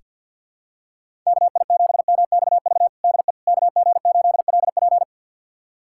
Morse code, heard as a keyed tone, about 48 wpm, 700 Hz. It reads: KI7MXV DE KG8LP
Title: USA Callsign DE Callsign 50wpm